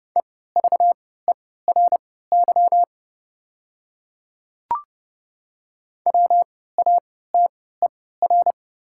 Morse code, heard as a keyed tone, about 28 wpm, spaced as for 17 wpm, 690 Hz. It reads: EVERY WATER
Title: 3x Character Spacing - Top 200 Words - 30wpm